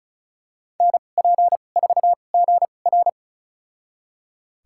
Morse code, heard as a keyed tone, about 34 wpm, spaced as for 25 wpm, 700 Hz. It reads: NP4GR
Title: USA Call Signs - 5 Characters - 2x Character Spacing - 35wpm